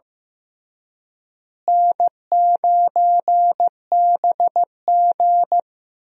Morse code, heard as a keyed tone, about 15 wpm, 700 Hz. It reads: N9BG
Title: USA Call Signs - 4 Characters - Repeated 3x - 15wpm